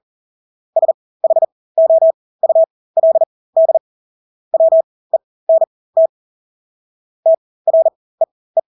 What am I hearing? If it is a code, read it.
Morse code, heard as a keyed tone, about 40 wpm, spaced as for 20 wpm, 650 Hz. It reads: SHOULD WENT TREE
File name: Engelske ord 40-20 wpm